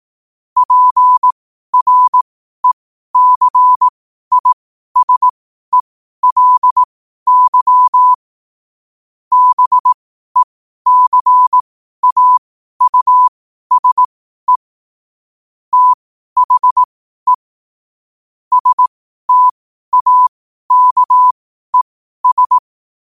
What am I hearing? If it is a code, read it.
Morse code, heard as a keyed tone, about 18 wpm, spaced as for 12 wpm, 1000 Hz. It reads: PRECISELY BECAUSE THE STAKES
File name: qotdfiles / 12wpm